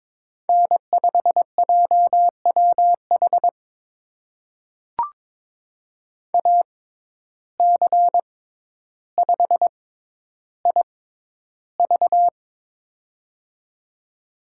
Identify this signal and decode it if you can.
Morse code, heard as a keyed tone, about 22 wpm, spaced as for 11 wpm, 700 Hz. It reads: N5JWH AC5IV
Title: USA Call Signs - 5 Characters - 6x Character Spacing - 22wpm